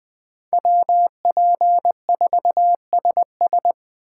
Morse code, heard as a keyed tone, about 20 wpm, 700 Hz. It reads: WP4SS
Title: USA Call Signs - 3x Character Spacing - 20wpm